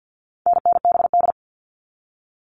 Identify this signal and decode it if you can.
Morse code, heard as a keyed tone, about 48 wpm, 700 Hz. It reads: NNBD
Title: All Cut Numbers - 4-Digit 50wpm